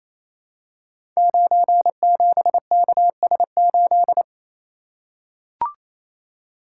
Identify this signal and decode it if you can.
Morse code, heard as a keyed tone, about 28 wpm, 700 Hz. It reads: 97KS8
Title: Code Groups - 5 Letter:Numbers - 28wpm